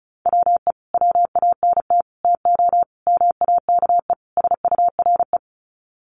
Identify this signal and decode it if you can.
Morse code, heard as a keyed tone, about 34 wpm, 700 Hz. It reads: WE WANT TO MAKE SURE
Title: Sets of 5 Words Encore2 35wpm